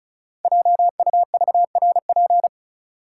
Morse code, heard as a keyed tone, about 35 wpm, 690 Hz. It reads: JUVRP
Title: Code Groups - 5 Letters - 35wpm